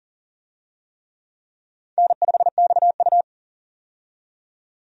Morse code, heard as a keyed tone, about 40 wpm, 700 Hz. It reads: N5XU
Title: USA Call Signs - 4 Characters - 40wpm